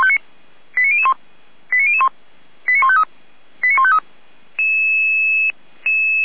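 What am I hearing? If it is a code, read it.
Five-tone selective call (ZVEI), heard as quick runs of five tones, tones of about 70 ms. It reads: 89001 89001 89153 89153
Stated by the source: Five-tone ZVEI selective call received by SDR.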